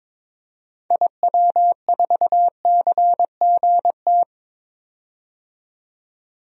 Morse code, heard as a keyed tone, about 22 wpm, 700 Hz. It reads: IW4CGT
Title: DXCC-100 Call Signs - 6 Characters - 22wpm